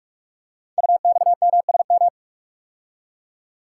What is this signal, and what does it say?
Morse code, heard as a keyed tone, about 44 wpm, 700 Hz. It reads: UXMSM
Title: Code Groups - 5 Letters - 45wpm